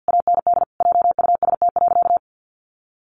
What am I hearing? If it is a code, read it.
Morse code, heard as a keyed tone, about 50 wpm, 700 Hz. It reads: AND JUST.
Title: Sentences from Top 800 Words 50wpm